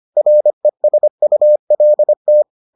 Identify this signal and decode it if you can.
Morse code, heard as a keyed tone, about 25 wpm, 600 Hz. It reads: RESULT